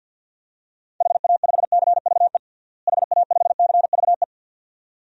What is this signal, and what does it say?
Morse code, heard as a keyed tone, about 50 wpm, 700 Hz. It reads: HA5CVE HA5CVE
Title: DXCC-100 Call Signs - 6 Characters - Repeated 2x - 50wpm